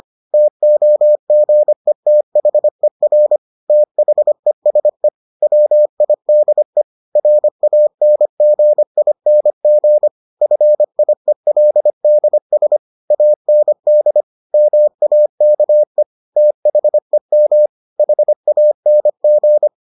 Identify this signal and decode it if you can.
Morse code, heard as a keyed tone, about 25 wpm, 600 Hz. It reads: TOGETHER THESE WIDE RANGING FIELDS AND MAKE THEM HANG